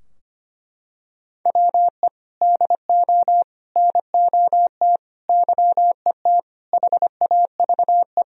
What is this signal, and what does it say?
Morse code, heard as a keyed tone, about 25 wpm, 700 Hz. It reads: WE DO NOT YET HAVE
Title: Sentences from Top 1000 Words - Review Part 3 - 25wpm